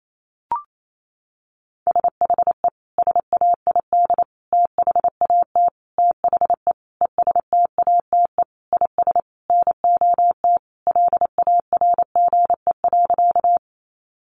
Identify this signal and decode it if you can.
Morse code, heard as a keyed tone, about 28 wpm, 700 Hz. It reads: SHE SAID THAT THE ESTATE IS NOT LARGE.
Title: Sentences from Top 1200 Words 28wpm